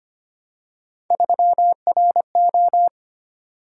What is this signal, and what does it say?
Morse code, heard as a keyed tone, about 25 wpm, 700 Hz. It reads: 3RO
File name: Three Letter-Number Combinations - 25wpm